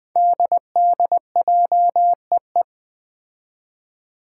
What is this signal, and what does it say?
Morse code, heard as a keyed tone, about 20 wpm, 700 Hz. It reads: DDJEE
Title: Five Challenging Letter-Number Combinations 20wpm